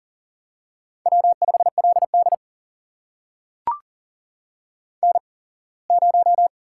Morse code, heard as a keyed tone, about 40 wpm, spaced as for 24 wpm, 700 Hz. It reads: W5LD N0
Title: USA Call Signs - 4 Characters - 8x Character Spacing - 40wpm